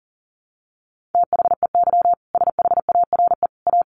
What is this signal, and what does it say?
Morse code, heard as a keyed tone, about 40 wpm, 700 Hz. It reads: THEY SHARE A
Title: Additional Easy Phrases 40wpm